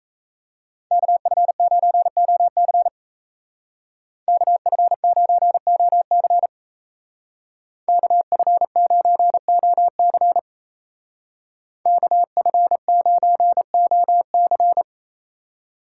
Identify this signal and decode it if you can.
Morse code, heard as a keyed tone, about 36 wpm, 700 Hz. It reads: KF9OC KF9OC KF9OC KF9OC